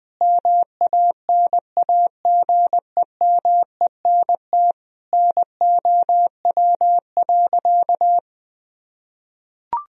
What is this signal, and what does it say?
Morse code, heard as a keyed tone, about 21 wpm, 700 Hz. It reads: MANAGEMENT NOW.
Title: Sentences from Top 1000 Words - Review Part 4 - 20wpm